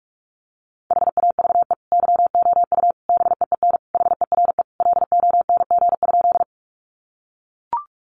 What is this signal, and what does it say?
Morse code, heard as a keyed tone, about 44 wpm, 700 Hz. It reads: HAVE YOU BEEN HERE LONG?
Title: Sentences from Top 200 Words 45wpm